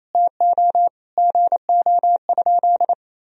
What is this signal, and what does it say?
Morse code, heard as a keyed tone, about 28 wpm, 700 Hz. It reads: TO GO?
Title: Sentences from Top 200 Words and Abbreviations 28wpm